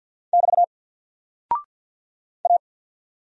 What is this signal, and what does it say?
Morse code, heard as a keyed tone, about 48 wpm, 700 Hz. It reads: <BT> A
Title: Single Character (Letter:Number:Punctuation:Prosign) Alt 50wpm